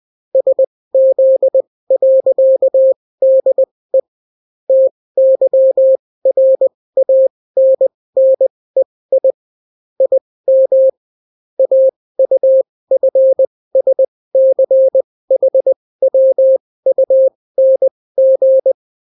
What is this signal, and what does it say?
Morse code, heard as a keyed tone, about 20 wpm, spaced as for 16 wpm, 530 Hz. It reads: SZ.DE TYRANNEI IM AUFSCHWUNG